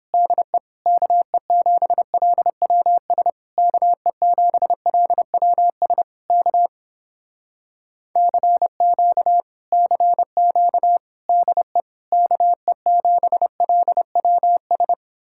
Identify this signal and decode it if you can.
Morse code, heard as a keyed tone, about 29 wpm, 700 Hz. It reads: DE KE7LWS KE7LWS K CQ CQ DE KE7LWS